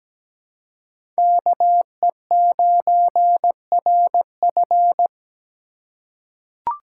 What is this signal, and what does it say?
Morse code, heard as a keyed tone, about 17 wpm, 700 Hz. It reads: KE9RF
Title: USA Call Signs - 3x Character Spacing - 17wpm